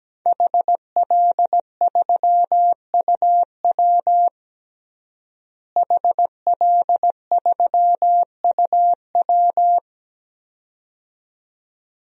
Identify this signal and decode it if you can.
Morse code, heard as a keyed tone, about 17 wpm, 700 Hz. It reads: HL3UW HL3UW
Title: DXCC-100 Call Signs - 5 Characters - Repeated 2x - 17wpm